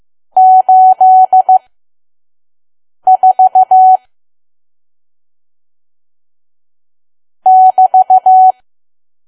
Morse code, heard as a keyed tone, about 15 wpm, spaced as for 5 wpm, 750 Hz. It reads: 84 <BT>
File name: Archive / 5 WPM files / PM